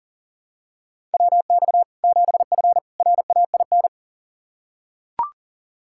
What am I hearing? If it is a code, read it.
Morse code, heard as a keyed tone, about 40 wpm, 700 Hz. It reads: WX 7F RAIN
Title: Practice QSO - Exchange 2 - Info - 40wpm